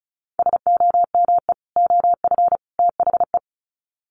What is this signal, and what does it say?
Morse code, heard as a keyed tone, about 34 wpm, 700 Hz. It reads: SOME OF THE
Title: Sets of 3 Words 35wpm